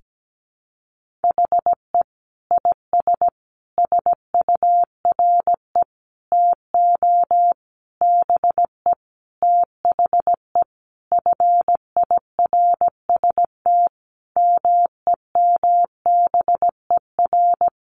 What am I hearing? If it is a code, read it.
Morse code, heard as a keyed tone, about 17 wpm, 700 Hz. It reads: HE IS SURE TO BE THE FIRST MEMBER
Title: Sentences from Top 1400 Words 17wpm